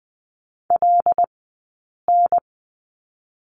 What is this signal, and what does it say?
Morse code, heard as a keyed tone, about 20 wpm, 700 Hz. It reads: L N
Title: 106 Lesson 18d - Letter m - Two Characters 20wpm